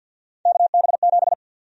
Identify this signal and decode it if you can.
Morse code, heard as a keyed tone, about 50 wpm, 700 Hz. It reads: KB7
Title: USA Call Sign Prefixes - Repeated 2x - 50wpm